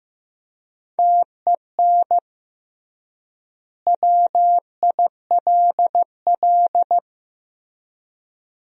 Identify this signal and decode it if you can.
Morse code, heard as a keyed tone, about 14 wpm, 700 Hz. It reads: TEN WILL
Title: 3x Word-Spacing - Sentences from Top 400 Words - 15wpm